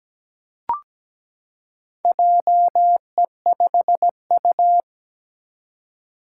Morse code, heard as a keyed tone, about 17 wpm, 700 Hz. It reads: JE5U
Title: Four Challenging Letter-Number Combinations 17wpm